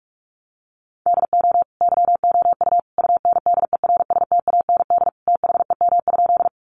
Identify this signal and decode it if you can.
Morse code, heard as a keyed tone, about 44 wpm, 700 Hz. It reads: DO YOU UNDERSTAND THEM?
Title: Sentences from Top 600 Words 45wpm